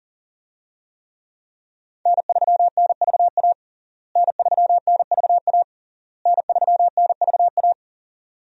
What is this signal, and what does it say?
Morse code, heard as a keyed tone, about 40 wpm, 700 Hz. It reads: N3NVA N3NVA N3NVA
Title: USA Call Signs - 5 Characters - Repeated 3x - 40wpm